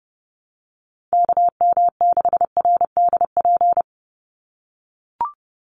Morse code, heard as a keyed tone, about 30 wpm, 700 Hz. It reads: KM6RDP
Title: USA Call Signs 30wpm